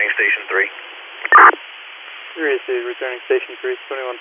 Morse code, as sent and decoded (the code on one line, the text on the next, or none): none